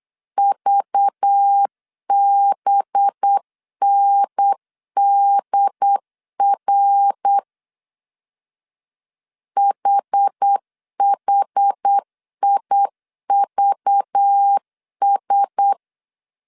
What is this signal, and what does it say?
Morse code, heard as a keyed tone, about 8 wpm, 800 Hz. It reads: VBNDR HHIVS
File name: 8wpm